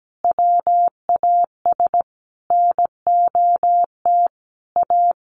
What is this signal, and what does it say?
Morse code, heard as a keyed tone, about 17 wpm, 700 Hz. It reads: WAS NOT A
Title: Sentences from Top 1500 Words 17wpm